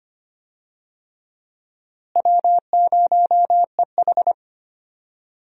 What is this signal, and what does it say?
Morse code, heard as a keyed tone, about 25 wpm, 700 Hz. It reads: W0EH
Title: USA Call Signs - 4 Characters - Repeated 3x - 25wpm